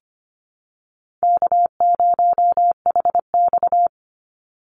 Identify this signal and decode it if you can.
Morse code, heard as a keyed tone, about 25 wpm, 700 Hz. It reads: K0HX
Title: Easy Call Signs Repeated 3x - 25wpm